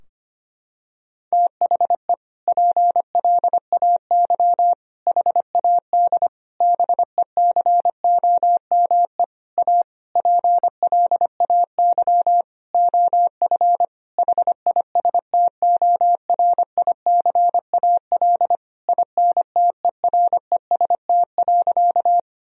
Morse code, heard as a keyed tone, about 25 wpm, 700 Hz. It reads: THE PLAY HAD BECOME A PLAY OF HISTORICAL INTEREST.
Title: Sentences from Top 1000 Words - Review Part 4 - 25wpm